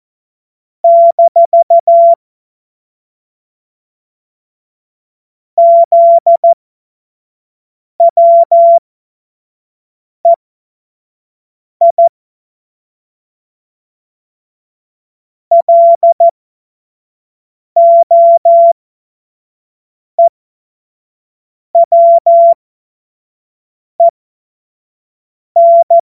Morse code, heard as a keyed tone, about 14 wpm, spaced as for 5 wpm, 680 Hz. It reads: - ZWEI LOEWEN
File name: Morsekurs / Witze / Witz 27. September 2022 25 F70